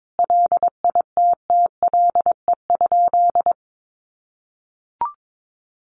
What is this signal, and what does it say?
Morse code, heard as a keyed tone, about 22 wpm, 700 Hz. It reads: LITTLE?
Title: Sentences from Top 900 Words 22wpm